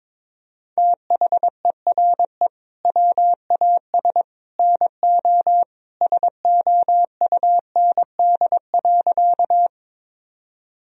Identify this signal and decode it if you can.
Morse code, heard as a keyed tone, about 22 wpm, 700 Hz. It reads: THERE WAS NO SOUND.
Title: Sentences from Top 1000 Words - Review Part 1 - 22wpm